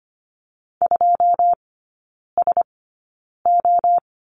Morse code, heard as a keyed tone, about 25 wpm, spaced as for 9 wpm, 690 Hz. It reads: 2SO
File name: Lesson 15e - Number 5 - Three Characters 25wpm